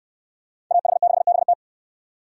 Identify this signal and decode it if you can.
Morse code, heard as a keyed tone, about 65 wpm, 700 Hz. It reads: AH6BT